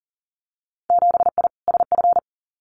Morse code, heard as a keyed tone, about 40 wpm, 700 Hz. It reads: 7I SF
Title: Winter Field Day Exchanges Encore 40wpm